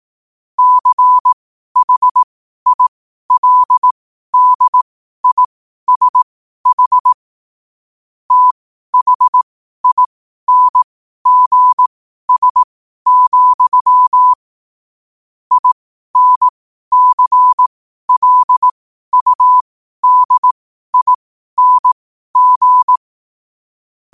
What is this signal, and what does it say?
Morse code, heard as a keyed tone, about 18 wpm, spaced as for 12 wpm, 1000 Hz. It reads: CHILDISH THINGS, INCLUDING
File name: qotdfiles / 12wpm